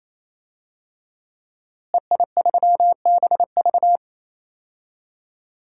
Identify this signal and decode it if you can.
Morse code, heard as a keyed tone, about 27 wpm, 700 Hz. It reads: EI3BV
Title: DXCC-100 Call Signs - 5 Characters - Repeated 3x - 28wpm